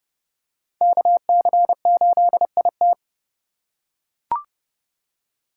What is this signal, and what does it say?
Morse code, heard as a keyed tone, about 32 wpm, 700 Hz. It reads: KC8IT
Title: USA Call Signs - 4x Character Spacing - 30wpm